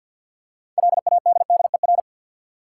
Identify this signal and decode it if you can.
Morse code, heard as a keyed tone, about 47 wpm, 700 Hz. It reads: LADDER
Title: Six Letter Words - 50wpm